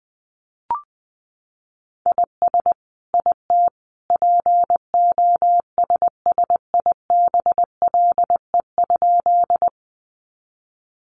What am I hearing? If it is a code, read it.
Morse code, heard as a keyed tone, about 20 wpm, 700 Hz. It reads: IS IT POSSIBLE?
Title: Sentences from Top 500 Words 20wpm